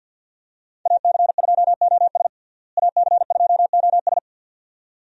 Morse code, heard as a keyed tone, about 51 wpm, 700 Hz. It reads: AC2OS AC2OS
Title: USA Call Signs - 5 Characters - Repeated 2x - 50wpm